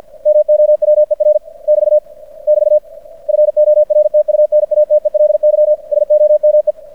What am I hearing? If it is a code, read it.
Morse code, heard as a keyed tone, about 50 wpm, 600 Hz. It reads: MOWER <BT> <BT> FORTUNATELY SOME
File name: archived / CRSnet